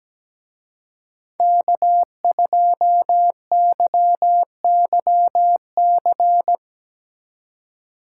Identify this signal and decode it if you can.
Morse code, heard as a keyed tone, about 19 wpm, 700 Hz. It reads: K2YYC
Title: Speed Racing - USA 1x3 Call Signs 17wpm